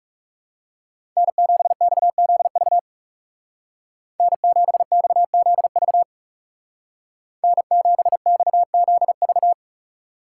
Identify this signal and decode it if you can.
Morse code, heard as a keyed tone, about 40 wpm, 700 Hz. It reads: N7XZV N7XZV N7XZV